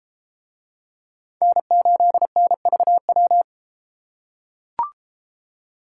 Morse code, heard as a keyed tone, about 30 wpm, 700 Hz. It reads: N8NVW